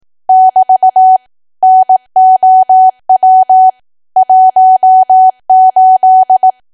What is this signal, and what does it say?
Morse code, heard as a keyed tone, about 18 wpm, 750 Hz. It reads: <BT> NOW 18
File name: cwa / W1AW-Level2 / 18WPM